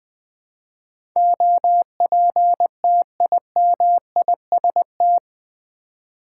Morse code, heard as a keyed tone, about 20 wpm, 700 Hz. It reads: OPTIMIST
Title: Build A Word - 4 Letters At A Time - 10-Letter Words - 20wpm